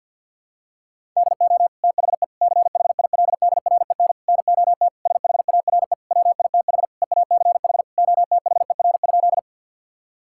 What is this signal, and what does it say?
Morse code, heard as a keyed tone, about 52 wpm, 700 Hz. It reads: DO THE CHILDREN NOT SHARE WITH EACH OTHER?
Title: Sentences from Top 1000 Words - Review Part 6 - 50wpm